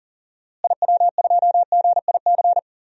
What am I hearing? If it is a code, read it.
Morse code, heard as a keyed tone, about 40 wpm, 700 Hz. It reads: IW2GIC
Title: DXCC-100 Call Signs - 6 Characters - Repeated 2x - 40wpm